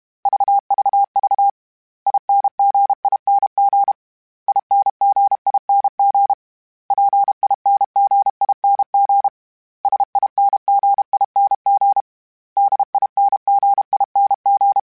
Morse code, heard as a keyed tone, about 32 wpm, 800 Hz. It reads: VVV INGING INGING PINGING SINGING DINGING
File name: Special INGs-mp3s / Special INGs-mp3s / Dble ings @30